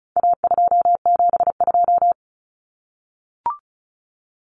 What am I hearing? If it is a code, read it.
Morse code, heard as a keyed tone, about 34 wpm, 700 Hz. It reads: A272
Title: Common Cut Numbers - 4-Digit 35wpm